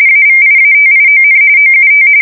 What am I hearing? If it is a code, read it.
RTTY, about 45.45 baud. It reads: AVIATION OUT
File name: fsk170 / usnavy5